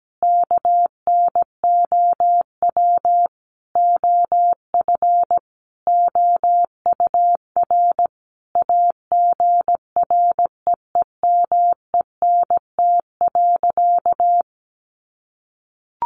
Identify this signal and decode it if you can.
Morse code, heard as a keyed tone, about 17 wpm, 700 Hz. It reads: KNOW OF OUR AGREEMENT.
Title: Sentences from Top 1500 Words 17wpm